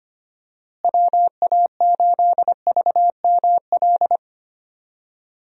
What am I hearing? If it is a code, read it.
Morse code, heard as a keyed tone, about 25 wpm, 700 Hz. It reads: WA8VML
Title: USA Call Signs - 6 Characters - Repeated 3x - 25wpm